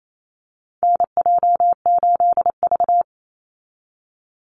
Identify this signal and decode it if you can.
Morse code, heard as a keyed tone, about 28 wpm, 700 Hz. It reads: NJ8V
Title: Call Signs Repeated 2x - 28wpm